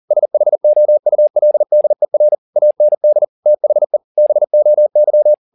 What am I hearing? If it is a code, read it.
Morse code, heard as a keyed tone, about 40 wpm, 600 Hz. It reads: SHOULDER AND THE BOY
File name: the old man and the sea 1 40wpm